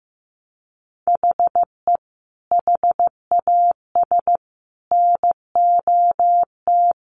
Morse code, heard as a keyed tone, about 15 wpm, 700 Hz. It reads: HE HAS NOT
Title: Sentences from Top 1100 Words 15wpm